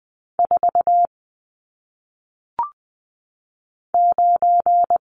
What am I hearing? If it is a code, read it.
Morse code, heard as a keyed tone, about 20 wpm, 700 Hz. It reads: 4 9
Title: Numbers - 1-Digit 20wpm